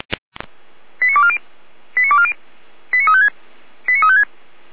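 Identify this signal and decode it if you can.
Five-tone selective call (ZVEI), heard as quick runs of five tones, tones of about 70 ms. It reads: 89240 89240 89356 89356
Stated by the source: ZVEI five-tone selective call 89192 received with an SDR.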